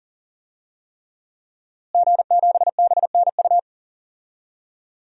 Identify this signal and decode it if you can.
Morse code, heard as a keyed tone, about 40 wpm, 700 Hz. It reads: G7BNU
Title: DXCC-100 Call Signs - 5 Characters - Repeated 3x - 40wpm